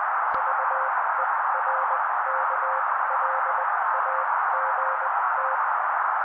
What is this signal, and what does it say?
Morse code, heard as a keyed tone, about 20 wpm, spaced as for 16 wpm, 530 Hz. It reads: VERKLAGT
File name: daily / 16wpm noise